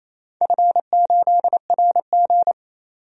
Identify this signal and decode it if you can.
Morse code, heard as a keyed tone, about 27 wpm, 690 Hz. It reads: F8RG
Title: DXCC-100 Call Signs - 4 Characters - 28wpm